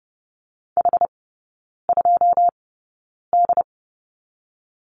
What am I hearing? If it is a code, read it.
Morse code, heard as a keyed tone, about 30 wpm, spaced as for 9 wpm, 700 Hz. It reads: H2D
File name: 081 Lesson 14e - Number 2 - Three Characters 30wpm